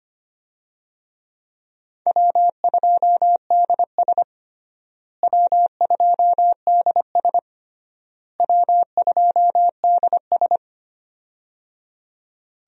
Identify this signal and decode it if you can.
Morse code, heard as a keyed tone, about 25 wpm, 700 Hz. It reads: W2DS W2DS W2DS
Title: USA Call Signs - 4 Characters - Repeated 3x - 25wpm